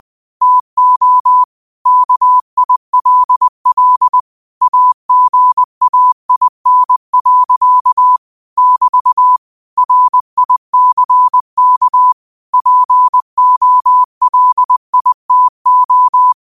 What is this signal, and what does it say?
Morse code, heard as a keyed tone, about 20 wpm, 1000 Hz. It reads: TO KILL AGAIN. <BT> RICK POLITO
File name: qotdfiles / 20wpm